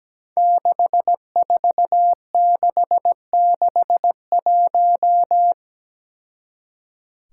Morse code, heard as a keyed tone, about 17 wpm, 700 Hz. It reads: 64661
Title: Five Challenging Letter-Number Combinations 17wpm